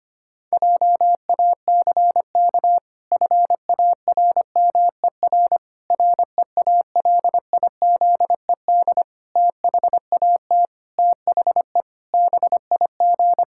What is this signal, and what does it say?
Morse code, heard as a keyed tone, about 25 wpm, 700 Hz. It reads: JACK FARMER REALIZED THAT THE BIG
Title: Pangram Challenge v2 - 25wpm